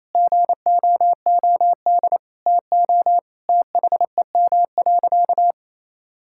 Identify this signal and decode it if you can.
Morse code, heard as a keyed tone, about 27 wpm, 700 Hz. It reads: GOOD TO THEM.
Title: Sentences from Top 1000 Words - Review Part 6 - 28wpm